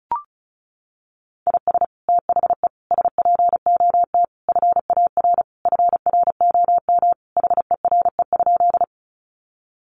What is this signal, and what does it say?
Morse code, heard as a keyed tone, about 34 wpm, 700 Hz. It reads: IS THE SPOT FAR FROM HERE?
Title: Sentences from Top 1100 Words 35wpm